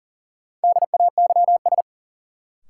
Morse code, heard as a keyed tone, about 40 wpm, 700 Hz. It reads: DAYS
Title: Sentences from Top 1000 Words - Review Part 5 - New Word Drill - 40wpm